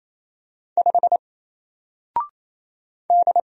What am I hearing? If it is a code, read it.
Morse code, heard as a keyed tone, about 27 wpm, 700 Hz. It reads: 5 D
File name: Single-Letter-Number-28wpm